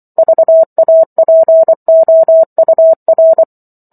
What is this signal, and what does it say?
Morse code, heard as a keyed tone, about 24 wpm, 650 Hz. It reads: VAPOUR
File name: Chapter 15 - 24 WPM